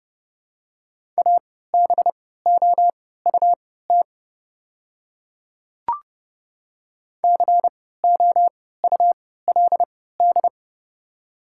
Morse code, heard as a keyed tone, about 30 wpm, spaced as for 17 wpm, 690 Hz. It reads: ABOUT COULD
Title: 3x Character Spacing - Top 100 Words - 30wpm